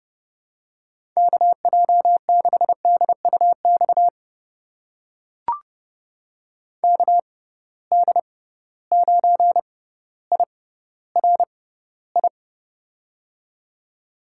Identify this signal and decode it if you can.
Morse code, heard as a keyed tone, about 30 wpm, spaced as for 15 wpm, 700 Hz. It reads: KJ6DUX KD9IRI
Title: USA Call Signs - 6 Characters - 6x Character Spacing - 30wpm